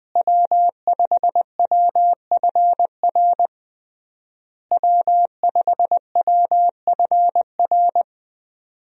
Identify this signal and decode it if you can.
Morse code, heard as a keyed tone, about 21 wpm, 700 Hz. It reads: W5WFR W5WFR
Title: USA Call Signs - 5 Characters - Repeated 3x - 20wpm